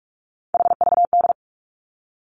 Morse code, heard as a keyed tone, about 46 wpm, 700 Hz. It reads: HVD
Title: Three Challenging Letter-Number Combinations 45wpm